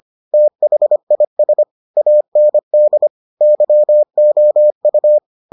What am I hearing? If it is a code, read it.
Morse code, heard as a keyed tone, about 25 wpm, 600 Hz. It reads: THIS AND YOU